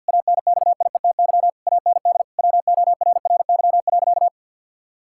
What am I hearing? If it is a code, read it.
Morse code, heard as a keyed tone, about 50 wpm, 700 Hz. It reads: ANXIETY AND WORRY.
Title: Pangram Challenge v2 - 50wpm